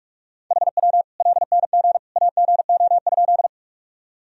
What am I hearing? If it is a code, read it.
Morse code, heard as a keyed tone, about 41 wpm, 700 Hz. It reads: HW LNG AGO?
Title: Sentences from Top 500 Words and Abbreviations 45wpm